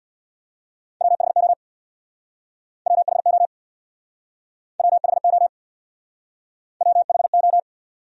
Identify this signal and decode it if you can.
Morse code, heard as a keyed tone, about 66 wpm, 700 Hz. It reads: WHO WHO WHO WHO